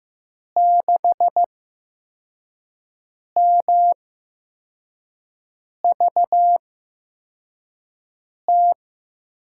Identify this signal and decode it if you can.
Morse code, heard as a keyed tone, about 15 wpm, spaced as for 4 wpm, 700 Hz. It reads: 6MVT
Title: USA Call Signs - 6 Characters - 8x Character Spacing - 15wpm